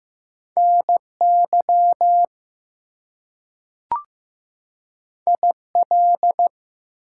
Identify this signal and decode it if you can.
Morse code, heard as a keyed tone, about 15 wpm, 700 Hz. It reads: NY IL
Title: US States and Canadian Provinces 15wpm